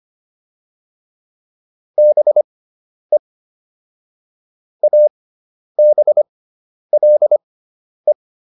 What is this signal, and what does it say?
Morse code, heard as a keyed tone, about 25 wpm, spaced as for 10 wpm, 600 Hz. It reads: BE ABLE